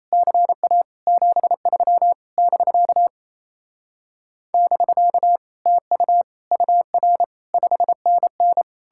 Morse code, heard as a keyed tone, about 36 wpm, 700 Hz. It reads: CA 73 <BK> <BK> TU UR 5NN